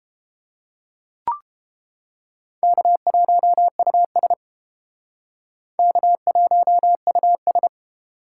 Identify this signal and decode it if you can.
Morse code, heard as a keyed tone, about 29 wpm, 700 Hz. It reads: K1US K1US